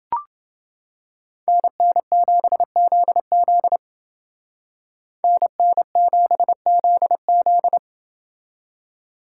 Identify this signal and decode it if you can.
Morse code, heard as a keyed tone, about 29 wpm, 700 Hz. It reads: NN7ZZ NN7ZZ